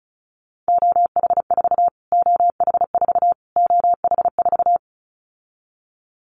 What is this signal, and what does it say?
Morse code, heard as a keyed tone, about 34 wpm, 700 Hz. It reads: OH4 OH4 OH4
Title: Callsign Prefixes for Top 25 DXCC - Repeated 3x - 35wpm